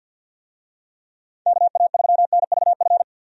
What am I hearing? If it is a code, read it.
Morse code, heard as a keyed tone, about 51 wpm, 700 Hz. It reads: KA3NVF
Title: USA Call Signs - 6 Characters - 50wpm